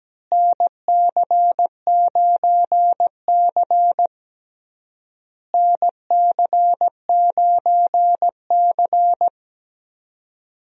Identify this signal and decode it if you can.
Morse code, heard as a keyed tone, about 17 wpm, 700 Hz. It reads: NC9C NC9C
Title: USA Call Signs - 4 Characters - Repeated 3x - 17wpm